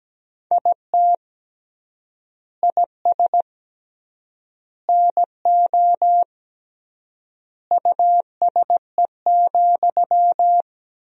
Morse code, heard as a keyed tone, about 17 wpm, 700 Hz. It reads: IT IS NO USE,
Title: 3x Word-Spacing - Sentences from Top 500 Words - 17wpm